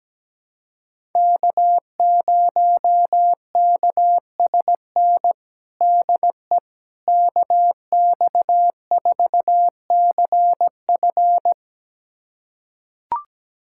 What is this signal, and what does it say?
Morse code, heard as a keyed tone, about 17 wpm, 700 Hz. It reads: K0KSN DE KX4CF
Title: USA Callsign DE Callsign 17wpm